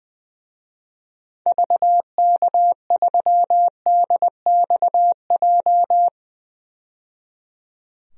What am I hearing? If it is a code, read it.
Morse code, heard as a keyed tone, about 20 wpm, 700 Hz. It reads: VK3DXJ
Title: DXCC-100 Call Signs - 6 Characters - 20wpm